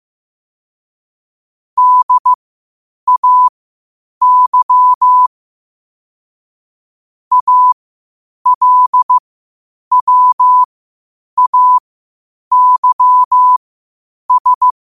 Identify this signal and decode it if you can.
Morse code, heard as a keyed tone, about 15 wpm, spaced as for 8 wpm, 1000 Hz. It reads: DAY ALWAYS
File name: qotdfiles / 8wpm